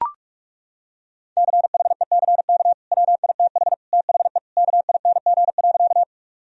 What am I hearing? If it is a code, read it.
Morse code, heard as a keyed tone, about 41 wpm, 700 Hz. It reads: CHECK WITH THE KING.
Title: Sentences from Top 1000 Words - Review Part 3 - 45wpm